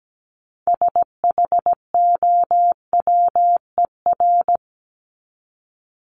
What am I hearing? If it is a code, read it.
Morse code, heard as a keyed tone, about 17 wpm, 700 Hz. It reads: SHOWER
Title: Top 1501-1750 Words 17wpm